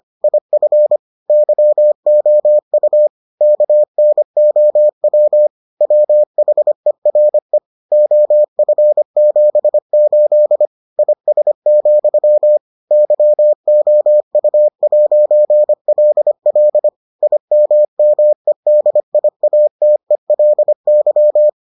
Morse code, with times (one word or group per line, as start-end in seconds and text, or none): 0.2-1.0 IF
1.3-3.1 YOU
3.4-5.5 KNOW
5.8-7.6 WHERE
7.9-10.7 OF78
11.0-12.6 IS,
12.9-16.9 YOU'LL
17.2-21.5 IMMEDIATELY